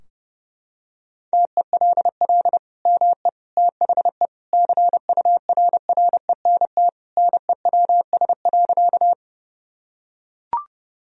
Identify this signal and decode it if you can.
Morse code, heard as a keyed tone, about 30 wpm, 700 Hz. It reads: TELL ME THE CURRENT NEWS.
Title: Sentences from Top 1000 Words - Review Part 4 - 30wpm